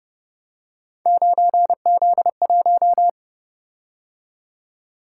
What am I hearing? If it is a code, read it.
Morse code, heard as a keyed tone, about 28 wpm, 700 Hz. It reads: 9Z1